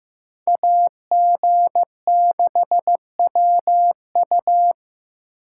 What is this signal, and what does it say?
Morse code, heard as a keyed tone, about 18 wpm, 700 Hz. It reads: AG6WU
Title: Speed Racing - USA Call Signs 15wpm